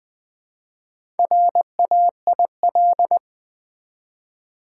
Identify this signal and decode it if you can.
Morse code, heard as a keyed tone, about 20 wpm, 700 Hz. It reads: RAIL
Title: Build A Word - 1 Letter At A Time - 4-Letter Words - 20wpm